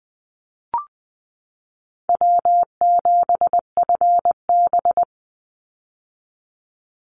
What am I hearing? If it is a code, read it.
Morse code, heard as a keyed tone, about 20 wpm, 700 Hz. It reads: W7FB
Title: Call Signs - 20wpm